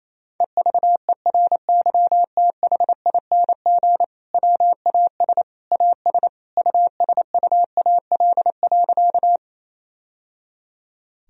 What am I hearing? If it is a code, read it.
Morse code, heard as a keyed tone, about 28 wpm, 700 Hz. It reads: EVERYTHING WAS AS USUAL.
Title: Sentences from Top 1000 Words - Review Part 2 - 28wpm